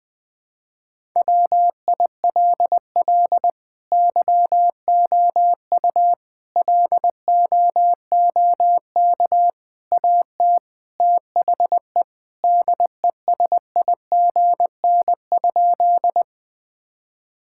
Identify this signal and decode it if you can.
Morse code, heard as a keyed tone, about 19 wpm, 700 Hz. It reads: WILL YOU LOOK AT THE DESIGN?
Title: Sentences from Top 1000 Words - Review Part 5 - 20wpm